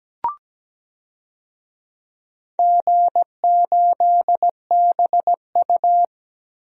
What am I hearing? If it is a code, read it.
Morse code, heard as a keyed tone, about 17 wpm, 700 Hz. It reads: G8BU
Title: DXCC-100 Call Signs - 4 Characters - Repeated 3x - 17wpm